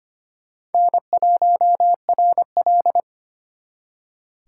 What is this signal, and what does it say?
Morse code, heard as a keyed tone, about 32 wpm, 700 Hz. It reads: N1RL